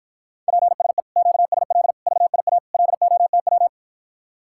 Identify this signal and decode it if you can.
Morse code, heard as a keyed tone, about 54 wpm, 700 Hz. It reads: PSE QSL VIA LOTW